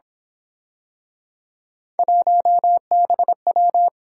Morse code, heard as a keyed tone, about 27 wpm, 700 Hz. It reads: 1BW
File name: Speed-Racing - Three Letter-Number Combinations - 22wpm